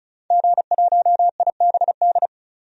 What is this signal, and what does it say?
Morse code, heard as a keyed tone, about 34 wpm, 700 Hz. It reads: G1IBD
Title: DXCC-100 Call Signs - 5 Characters - 35wpm